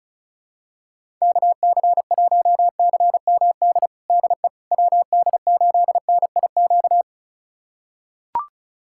Morse code, heard as a keyed tone, about 36 wpm, 700 Hz. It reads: KC1CMD DE WD8NIQ
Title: USA Callsign DE Callsign 35wpm